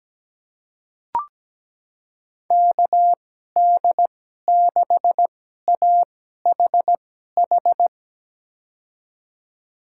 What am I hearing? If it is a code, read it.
Morse code, heard as a keyed tone, about 17 wpm, spaced as for 12 wpm, 700 Hz. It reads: KD6AHH
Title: USA Call Signs - 6 Characters - 2x Character Spacing - 17wpm